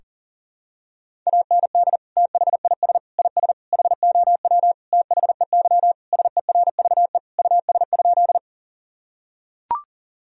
Morse code, heard as a keyed tone, about 40 wpm, 700 Hz. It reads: AND THIS IS HOW THEY SERVE US?
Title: Sentences from Top 1000 Words - Review Part 2 - 40wpm